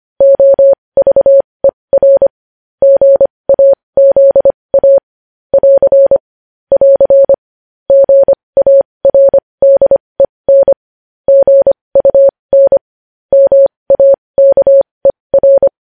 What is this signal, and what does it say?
Morse code, heard as a keyed tone, about 25 wpm, spaced as for 20 wpm, 550 Hz. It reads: OVER GAZA <AR> <AR> GARDEN GUN MAKER
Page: News Headlines in Morse Code at 20 WPM